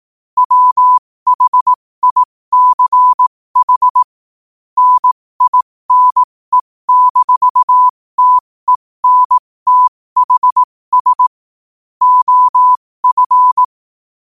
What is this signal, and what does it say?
Morse code, heard as a keyed tone, about 18 wpm, spaced as for 15 wpm, 1000 Hz. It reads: WHICH NINE-TENTHS OF
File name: qotdfiles / 15wpm